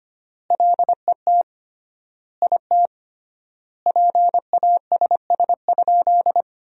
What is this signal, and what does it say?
Morse code, heard as a keyed tone, about 25 wpm, 700 Hz. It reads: LET IT PASS?
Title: 3x Word-Spacing - Sentences from Top 400 Words - 25wpm